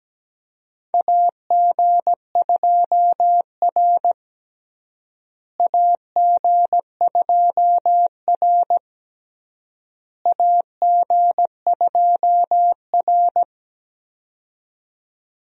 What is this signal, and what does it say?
Morse code, heard as a keyed tone, about 17 wpm, 700 Hz. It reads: AG2R AG2R AG2R
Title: USA Call Signs - 4 Characters - Repeated 3x - 17wpm